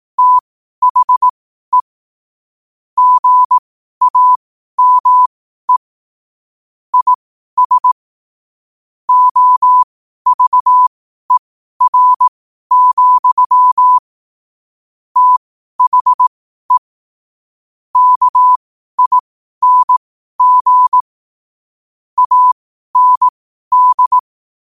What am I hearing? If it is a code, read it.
Morse code, heard as a keyed tone, about 18 wpm, spaced as for 12 wpm, 1000 Hz. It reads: THE GAME IS OVER, THE KING AND
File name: qotdfiles / 12wpm